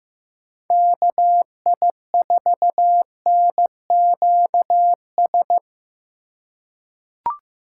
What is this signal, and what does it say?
Morse code, heard as a keyed tone, about 15 wpm, 700 Hz. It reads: KI4NQS
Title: USA Call Signs - 6 Characters - 2x Character Spacing - 15wpm